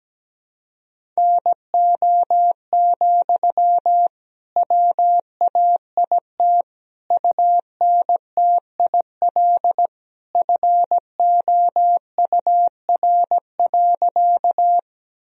Morse code, heard as a keyed tone, about 17 wpm, 700 Hz. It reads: NO, WAIT UNTIL FOUR.
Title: Sentences from Top 1000 Words - Review Part 1 - 17wpm